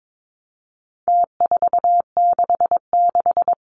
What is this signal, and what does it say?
Morse code, heard as a keyed tone, about 22 wpm, 700 Hz. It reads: T466
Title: Numbers - Contest Style - 4-Digit 22wpm